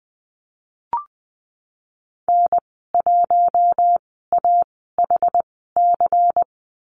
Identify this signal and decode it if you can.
Morse code, heard as a keyed tone, about 20 wpm, spaced as for 14 wpm, 700 Hz. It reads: N1AHC
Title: 147 Lesson 24g - Letter y - Call Signs 20wpm